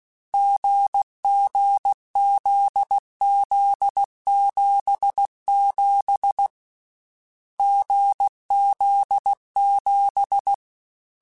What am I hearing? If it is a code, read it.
Morse code, heard as a keyed tone, about 16 wpm, 800 Hz. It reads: GGZZ77 GZ7
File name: Difficult Combos / Difficult Combos-mp3s / GZ GZ7 @15